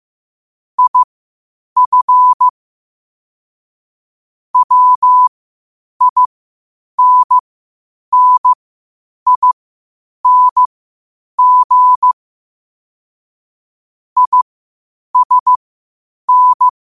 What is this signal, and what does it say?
Morse code, heard as a keyed tone, about 15 wpm, spaced as for 8 wpm, 1000 Hz. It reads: IF WINNING ISN
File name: qotdfiles / 8wpm